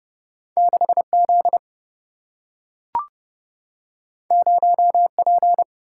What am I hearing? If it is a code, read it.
Morse code, heard as a keyed tone, about 31 wpm, 700 Hz. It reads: 6Z 0P
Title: Single-Letter Single-Number 30wpm